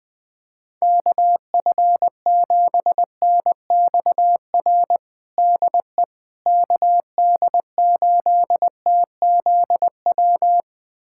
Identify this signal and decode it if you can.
Morse code, heard as a keyed tone, about 21 wpm, 700 Hz. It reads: KF7NXR DE KD8TZW
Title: USA Callsign DE Callsign - 6 Characters - 20wpm